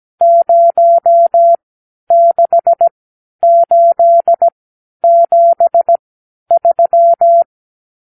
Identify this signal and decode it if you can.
Morse code, heard as a keyed tone, about 17 wpm, spaced as for 10 wpm, 680 Hz. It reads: 06873
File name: arrl / Review (1-0)